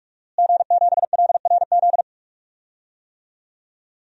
Morse code, heard as a keyed tone, about 45 wpm, 700 Hz. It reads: G7LRZ
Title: DXCC-100 Call Signs - 5 Characters - Repeated 2x - 45wpm